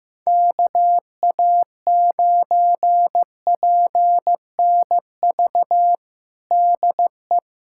Morse code, heard as a keyed tone, about 15 wpm, 700 Hz. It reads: KA9PNV DE
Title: USA Callsign DE Callsign 15wpm